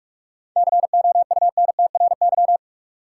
Morse code, heard as a keyed tone, about 46 wpm, 700 Hz. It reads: COUNTRY
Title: Sentences from Top 300 Words - New Word Drill - 45wpm